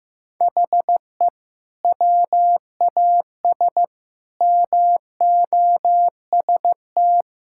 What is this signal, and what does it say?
Morse code, heard as a keyed tone, about 15 wpm, 700 Hz. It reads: HE WAS MOST
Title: Sentences from Top 1000 Words - Review Part 6 - 15wpm